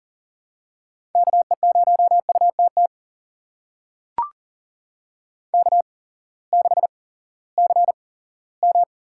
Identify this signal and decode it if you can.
Morse code, heard as a keyed tone, about 43 wpm, spaced as for 20 wpm, 690 Hz. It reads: KE0UTT K6CM
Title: USA Call Signs - 8x Character Spacing - 40wpm